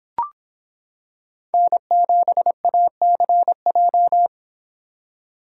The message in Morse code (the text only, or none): N7ACJ